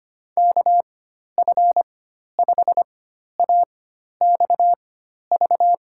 Morse code, heard as a keyed tone, about 25 wpm, spaced as for 12 wpm, 700 Hz. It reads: KF5AXV
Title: USA Call Signs - 6 Characters - 4x Character Spacing - 25wpm